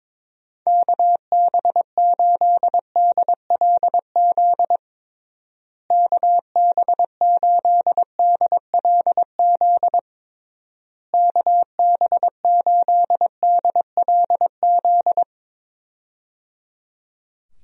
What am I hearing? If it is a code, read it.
Morse code, heard as a keyed tone, about 21 wpm, 700 Hz. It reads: KB8DLZ KB8DLZ KB8DLZ
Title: USA Call Signs - 6 Characters - Repeated 3x - 22wpm